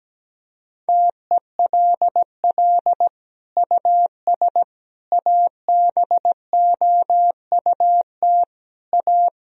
Morse code, heard as a keyed tone, about 17 wpm, 700 Hz. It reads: TELL US ABOUT A
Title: Sentences from Top 1000 Words - Review Part 3 - 17wpm